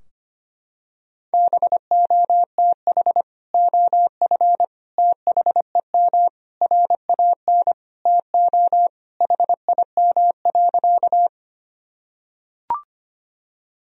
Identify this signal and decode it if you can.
Morse code, heard as a keyed tone, about 25 wpm, 700 Hz. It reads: BOTH OF THEM RAN TO HIM.
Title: Sentences from Top 1000 Words - Review Part 3 - 25wpm